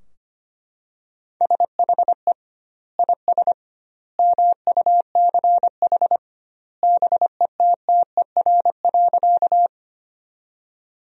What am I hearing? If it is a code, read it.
Morse code, heard as a keyed tone, about 25 wpm, 700 Hz. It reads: SHE IS MUCH BETTER.
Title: 2x Word-Spacing - Sentences from Top 400 Words - 25wpm